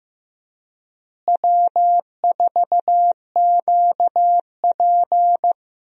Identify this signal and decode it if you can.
Morse code, heard as a keyed tone, about 16 wpm, 700 Hz. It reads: W4QP
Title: USA Call Signs - 4 Characters - Repeated 3x - 15wpm